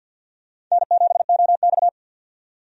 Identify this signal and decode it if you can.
Morse code, heard as a keyed tone, about 52 wpm, 700 Hz. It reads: N7OX
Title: USA Call Signs - 4 Characters - 50wpm